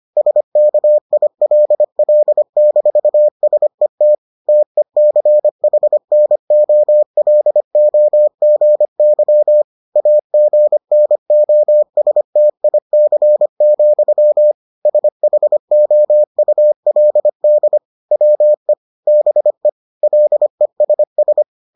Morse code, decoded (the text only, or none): SKILL-SET TECHNOLOGY AGNOSTIC, SHOULD WE BE LESS